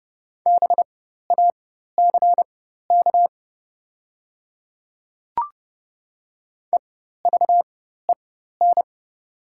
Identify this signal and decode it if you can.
Morse code, heard as a keyed tone, about 30 wpm, spaced as for 14 wpm, 700 Hz. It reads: BACK EVEN
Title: 4x Character Spacing - Top 300 Words - 30wpm